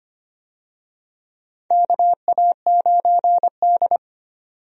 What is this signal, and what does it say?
Morse code, heard as a keyed tone, about 25 wpm, 700 Hz. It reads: KA9D
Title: USA Call Signs - 4 Characters - 25wpm